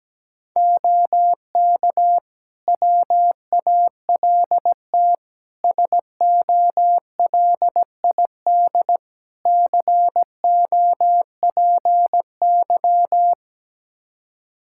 Morse code, heard as a keyed tone, about 17 wpm, 700 Hz. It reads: OK WALT SOLID COPY
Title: Practice QSO - Exchange 3 - More Info - 17wpm